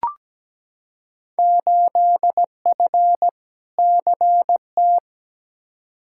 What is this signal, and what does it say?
Morse code, heard as a keyed tone, about 17 wpm, 700 Hz. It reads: 8F CT
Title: Field Day Exchanges Encore - 17wpm